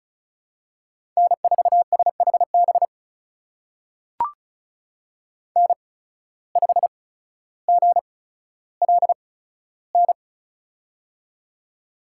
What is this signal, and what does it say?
Morse code, heard as a keyed tone, about 36 wpm, spaced as for 15 wpm, 700 Hz. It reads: N4SHB N5GLN
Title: USA Call Signs - 5 Characters - 8x Character Spacing - 35wpm